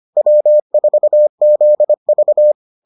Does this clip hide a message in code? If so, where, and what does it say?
Yes: Morse code from 0.2 to 2.5 s, heard as a keyed tone, about 25 wpm, 600 Hz. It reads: W4ZV